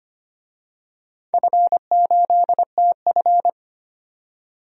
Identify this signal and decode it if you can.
Morse code, heard as a keyed tone, about 25 wpm, 700 Hz. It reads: F8TF
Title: DXCC-100 Call Signs - 4 Characters - 25wpm